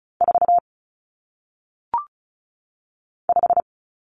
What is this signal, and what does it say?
Morse code, heard as a keyed tone, about 36 wpm, 700 Hz. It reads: 4 5
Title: Numbers - 1-Digit 35wpm